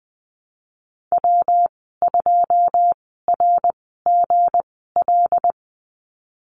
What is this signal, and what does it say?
Morse code, heard as a keyed tone, about 20 wpm, spaced as for 15 wpm, 700 Hz. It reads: W2RGL
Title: 161 Lesson 26g - Letter g - Call Signs 20wpm